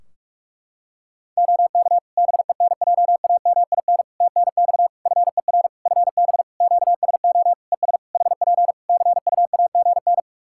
Morse code, heard as a keyed tone, about 41 wpm, 700 Hz. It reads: OK BENJAMIN TNX FER FB QSO ES HP CUAGN
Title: Practice QSO - Exchange 4 - Ending - 45wpm